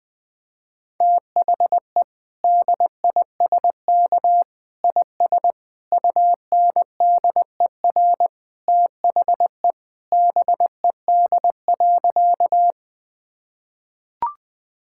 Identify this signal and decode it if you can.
Morse code, heard as a keyed tone, about 20 wpm, 700 Hz. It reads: THE DISK IS UNDER THE BED.
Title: Sentences from Top 1000 Words - Review Part 6 - 20wpm